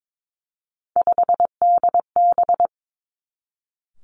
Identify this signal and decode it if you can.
Morse code, heard as a keyed tone, about 22 wpm, 700 Hz. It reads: 5DB
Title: Three Challenging Letter-Number Combinations 22wpm